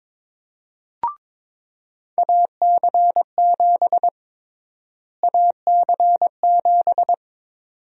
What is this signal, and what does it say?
Morse code, heard as a keyed tone, about 21 wpm, 700 Hz. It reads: AC7 AC7
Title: USA Call Sign Prefixes - Repeated 2x - 22wpm